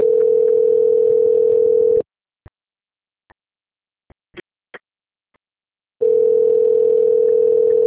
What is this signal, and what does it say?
Ringback tone (North America), 440 + 480 Hz.
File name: out- - - - - .7677